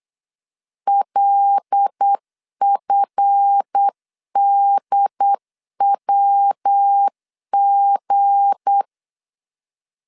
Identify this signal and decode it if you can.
Morse code, heard as a keyed tone, about 8 wpm, 800 Hz. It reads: LFDWG
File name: LEZIONI / 7 GF9Z / 7 GF 8wpm 3108